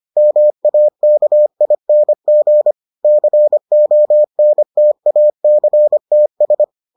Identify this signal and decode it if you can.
Morse code, heard as a keyed tone, about 25 wpm, 600 Hz. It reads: MAKING CONTACTS